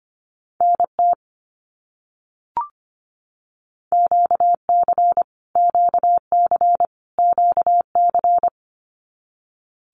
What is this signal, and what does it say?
Morse code, heard as a keyed tone, about 25 wpm, 700 Hz. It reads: NT QC QC QC
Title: Canadian Provinces and Territories - Repeated 3x - 25wpm